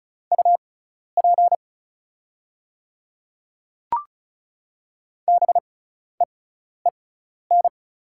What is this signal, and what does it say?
Morse code, heard as a keyed tone, about 36 wpm, spaced as for 12 wpm, 700 Hz. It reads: UP BEEN
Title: 6x Character Spacing - Top 200 Words - 35wpm